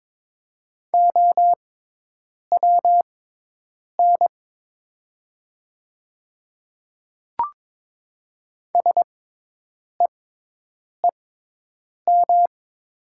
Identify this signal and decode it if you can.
Morse code, heard as a keyed tone, about 22 wpm, spaced as for 8 wpm, 700 Hz. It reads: OWN SEEM
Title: 6x Character Spacing - Top 300 Words - 22wpm